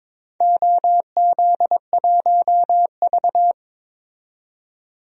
Morse code, heard as a keyed tone, about 22 wpm, 700 Hz. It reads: OZ1V
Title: DXCC-100 Call Signs - 4 Characters - 22wpm